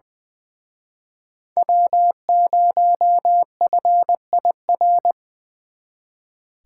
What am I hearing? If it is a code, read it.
Morse code, heard as a keyed tone, about 25 wpm, 700 Hz. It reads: W0FIR
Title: Speed Racing - USA 1x3 Call Signs 20wpm